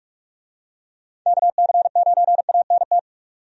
Morse code, heard as a keyed tone, about 46 wpm, 700 Hz. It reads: KC9ANT
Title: USA Call Signs - 6 Characters - 45wpm